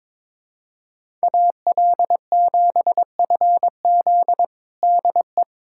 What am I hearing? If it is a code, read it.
Morse code, heard as a keyed tone, about 22 wpm, 700 Hz. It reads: AL7FZ DE
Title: USA Callsign DE Callsign - 5 Characters - 22wpm